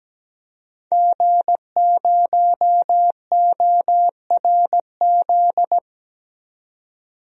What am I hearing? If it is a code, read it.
Morse code, heard as a keyed tone, about 17 wpm, 700 Hz. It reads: G0ORZ
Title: DXCC-100 Call Signs - 5 Characters - 17wpm